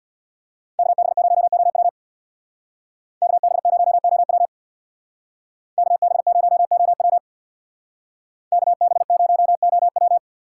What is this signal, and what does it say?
Morse code, heard as a keyed tone, about 56 wpm, 700 Hz. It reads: KB0OW KB0OW KB0OW KB0OW
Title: Speed Racing - USA 2x2 Call Signs 50wpm